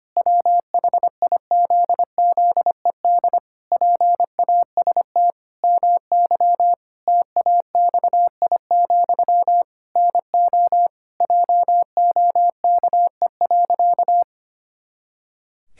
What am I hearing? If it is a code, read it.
Morse code, heard as a keyed tone, about 25 wpm, 700 Hz. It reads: WHIZZED PAST MY TAXI, NO JOKE.
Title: Pangram Challenge v2 - 25wpm